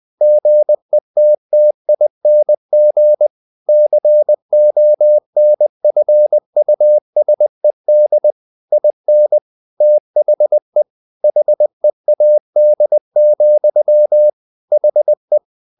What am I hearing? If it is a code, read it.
Morse code, heard as a keyed tone, about 20 wpm, 600 Hz. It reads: GETTING CONFUSED IN THE HEAD, HE
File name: the old man and the sea part 2 20wpm